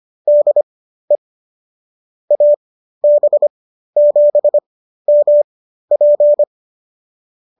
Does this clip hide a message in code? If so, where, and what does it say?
Yes: Morse code from 0.3 to 6.4 s, heard as a keyed tone, about 25 wpm, spaced as for 13 wpm, 590 Hz. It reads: DE AB7MP